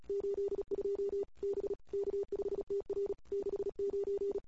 Morse code, heard as a keyed tone, about 34 wpm, 400 Hz. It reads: 82 B K5TR 69
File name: CW / 2008-ss-cw-k5tr-op-n5rz -21:00